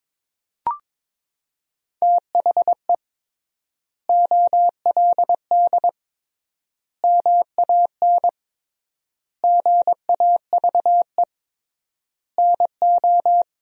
Morse code, heard as a keyed tone, about 22 wpm, 700 Hz. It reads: THE OLD MAN GAVE NO
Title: 3x Word-Spacing - Sentences from Top 500 Words - 22wpm